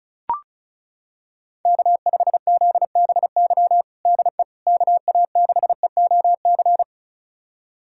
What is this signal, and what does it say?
Morse code, heard as a keyed tone, about 36 wpm, 700 Hz. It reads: K5ZBY DE KA6EOC
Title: USA Callsign DE Callsign 35wpm